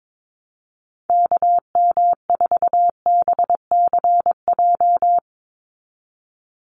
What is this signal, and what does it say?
Morse code, heard as a keyed tone, about 22 wpm, 700 Hz. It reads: KM4BCJ
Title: USA Call Signs 22wpm